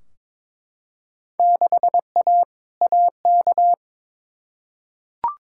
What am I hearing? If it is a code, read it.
Morse code, heard as a keyed tone, about 22 wpm, 700 Hz. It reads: 6A AK
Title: Field Day Exchanges Encore - 22wpm